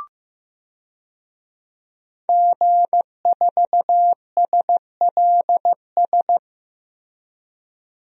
Morse code, heard as a keyed tone, about 15 wpm, 700 Hz. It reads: G4SLS
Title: DXCC-100 Call Signs - 5 Characters - Repeated 3x - 15wpm